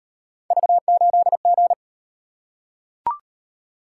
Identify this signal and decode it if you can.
Morse code, heard as a keyed tone, about 38 wpm, 700 Hz. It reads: V8G